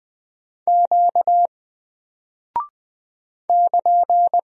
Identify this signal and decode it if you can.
Morse code, heard as a keyed tone, about 20 wpm, 700 Hz. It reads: Q <KN>
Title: Single Character (Letter:Number:Punctuation:Prosign) 20wpm